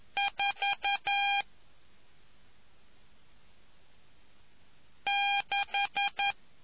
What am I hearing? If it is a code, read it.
Morse code, heard as a keyed tone, about 11 wpm, 800 Hz. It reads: 4 6